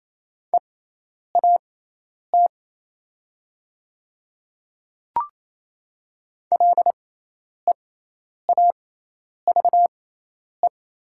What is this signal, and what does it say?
Morse code, heard as a keyed tone, about 28 wpm, spaced as for 10 wpm, 700 Hz. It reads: EAT LEAVE
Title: 6x Character Spacing - Top 500 Words - 28wpm